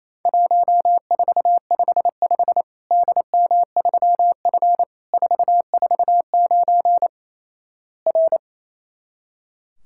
Morse code, heard as a keyed tone, about 28 wpm, 700 Hz. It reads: 1455 DM3F 449 R
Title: Worked All Europe DX Contest - QTC Exchange - 28wpm